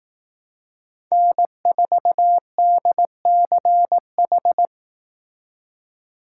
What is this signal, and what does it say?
Morse code, heard as a keyed tone, about 18 wpm, 700 Hz. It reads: N4DCH
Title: Speed Racing - USA 1x3 Call Signs 15wpm